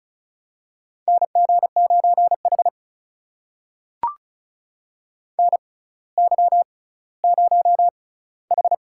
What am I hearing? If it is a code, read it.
Morse code, heard as a keyed tone, about 36 wpm, spaced as for 18 wpm, 700 Hz. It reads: NG9H NY0H
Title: USA Call Signs - 4 Characters - 6x Character Spacing - 35wpm